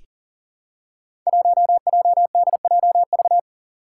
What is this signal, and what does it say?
Morse code, heard as a keyed tone, about 40 wpm, 700 Hz. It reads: 1JDJV
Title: Five Challenging Letter-Number Combinations 40wpm